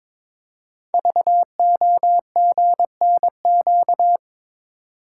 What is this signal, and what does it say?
Morse code, heard as a keyed tone, about 23 wpm, 700 Hz. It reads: VOGNQ
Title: Code Groups - 5 Letters - 22wpm